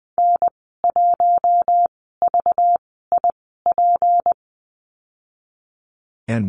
Morse code, heard as a keyed tone, about 20 wpm, 700 Hz. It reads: N 1 V I P
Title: 231 Lesson 37g - Letter x - Call Signs 20wpm